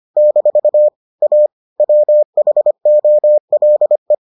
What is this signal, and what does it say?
Morse code, heard as a keyed tone, about 25 wpm, 600 Hz. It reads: - A WHOLE